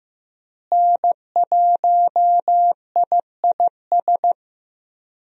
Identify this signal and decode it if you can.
Morse code, heard as a keyed tone, about 15 wpm, 700 Hz. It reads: N1IIS
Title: USA Call Signs - 5 Characters - Repeated 3x - 15wpm